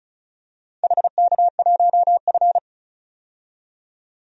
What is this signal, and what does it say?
Morse code, heard as a keyed tone, about 36 wpm, 700 Hz. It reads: HK1F
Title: DXCC-100 Call Signs - 4 Characters - 35wpm